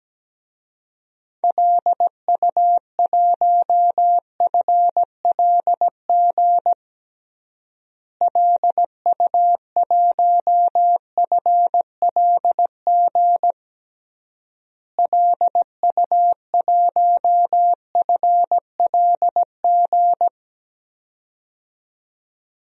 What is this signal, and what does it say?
Morse code, heard as a keyed tone, about 17 wpm, 700 Hz. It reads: LU1FLG LU1FLG LU1FLG
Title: DXCC-100 Call Signs - 6 Characters - Repeated 3x - 17wpm